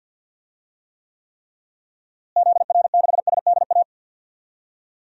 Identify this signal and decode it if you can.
Morse code, heard as a keyed tone, about 51 wpm, 700 Hz. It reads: ZR6SDA
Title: DXCC-100 Call Signs - 6 Characters - Repeated 3x - 50wpm